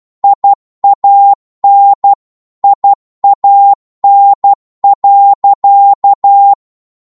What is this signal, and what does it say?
Morse code, heard as a keyed tone, about 12 wpm, 800 Hz. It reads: IAN IAN.
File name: Morse / qso-12wpm